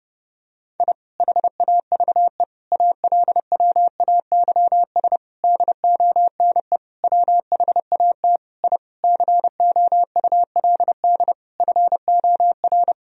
Morse code, heard as a keyed tone, about 32 wpm, 700 Hz. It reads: I HAVE ALWAYS DONE WHAT I COULD FOR
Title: Sentences from Top 1000 Words - Review Part 2 - 30wpm